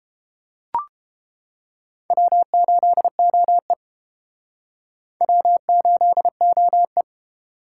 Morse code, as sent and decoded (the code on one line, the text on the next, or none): .-- ---.. --- . / .-- ---.. --- .
W8OE W8OE